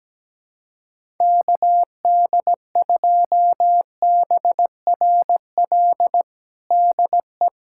Morse code, heard as a keyed tone, about 17 wpm, 700 Hz. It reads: KD2BRL DE
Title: USA Callsign DE Callsign 17wpm